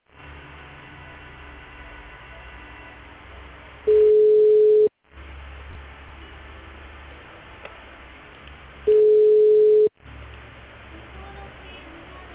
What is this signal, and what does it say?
Ringback tone (Europe, 425 Hz).